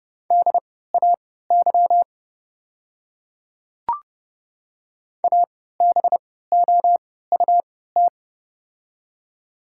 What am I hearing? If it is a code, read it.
Morse code, heard as a keyed tone, about 29 wpm, spaced as for 17 wpm, 700 Hz. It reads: DAY ABOUT
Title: 3x Character Spacing - Top 100 Words - 30wpm